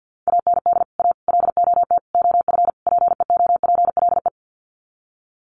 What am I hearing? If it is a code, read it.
Morse code, heard as a keyed tone, about 50 wpm, 700 Hz. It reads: AND A LOT OF PEOPLE
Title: Sets of 5 Words 50wpm